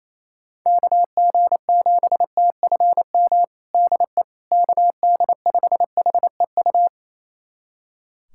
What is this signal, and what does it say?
Morse code, heard as a keyed tone, about 28 wpm, 700 Hz. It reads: KG7TFM DE KD5HEU
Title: USA Callsign DE Callsign - 6 Characters - 28wpm